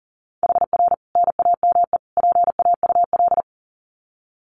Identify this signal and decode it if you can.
Morse code, heard as a keyed tone, about 40 wpm, 700 Hz. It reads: HR NAME PAUL
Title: Common QSO Phrases 40wpm